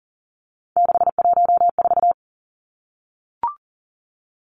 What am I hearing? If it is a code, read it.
Morse code, heard as a keyed tone, about 40 wpm, 700 Hz. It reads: 614
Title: Numbers - 3-Digit 40wpm